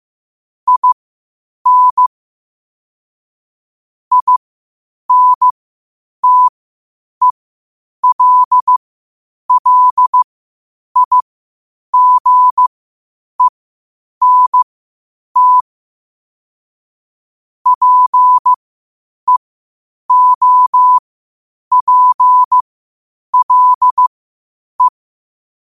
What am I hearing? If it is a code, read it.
Morse code, heard as a keyed tone, about 15 wpm, spaced as for 8 wpm, 1000 Hz. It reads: IN INTELLIGENT PEOPLE
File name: qotdfiles / 8wpm